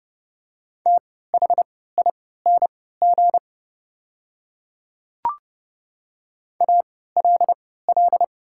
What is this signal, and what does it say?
Morse code, heard as a keyed tone, about 30 wpm, spaced as for 17 wpm, 700 Hz. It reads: THING ALL
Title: 3x Character Spacing - Top 100 Words - 30wpm